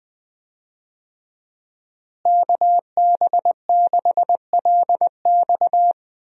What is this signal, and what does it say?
Morse code, heard as a keyed tone, about 20 wpm, 700 Hz. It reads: KB6LX
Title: USA Call Signs - 5 Characters - Repeated 2x - 20wpm